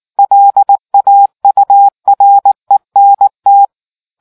Morse code, heard as a keyed tone, about 19 wpm, 800 Hz. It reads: LAURENT